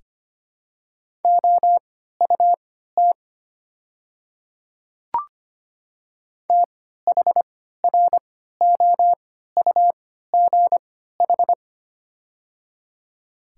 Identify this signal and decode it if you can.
Morse code, heard as a keyed tone, about 25 wpm, spaced as for 14 wpm, 690 Hz. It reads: OUT THROUGH
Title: 3x Character Spacing - Top 200 Words - 25wpm